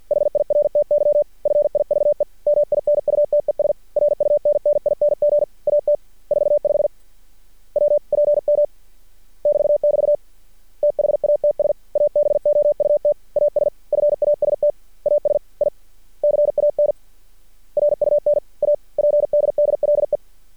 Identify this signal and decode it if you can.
Morse code, heard as a keyed tone, about 50 wpm, 590 Hz. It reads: HIRTY FIVE MINUTES RUNNING AT 45 WPM <BT><BT> THATS ABOUT AS FAST AS I CAN RUN A PDDLE
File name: archived / CRSnet